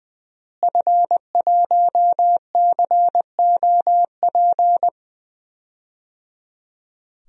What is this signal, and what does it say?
Morse code, heard as a keyed tone, about 20 wpm, 700 Hz. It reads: F1COP
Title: DXCC-100 Call Signs - 5 Characters - Repeated 3x - 20wpm